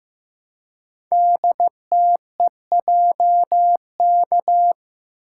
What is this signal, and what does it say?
Morse code, heard as a keyed tone, about 15 wpm, 700 Hz. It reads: DTEJK
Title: Code Groups - 5 Letters - 15wpm